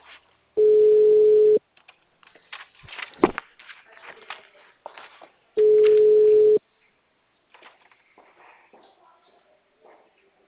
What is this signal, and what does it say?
Ringback tone (Europe, 425 Hz).